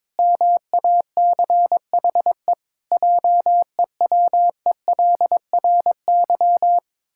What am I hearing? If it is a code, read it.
Morse code, heard as a keyed tone, about 22 wpm, 700 Hz. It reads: MACHE JEWELRY
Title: Pangram Challenge v2 - 22wpm